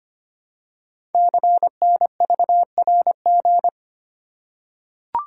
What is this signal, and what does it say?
Morse code, heard as a keyed tone, about 25 wpm, 700 Hz. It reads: CNVRG
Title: Code Groups - 5 Letters - 25wpm